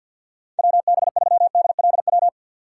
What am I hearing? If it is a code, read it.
Morse code, heard as a keyed tone, about 52 wpm, 700 Hz. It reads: WB3DLW
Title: USA Call Signs - 6 Characters - 6x Character Spacing - 50wpm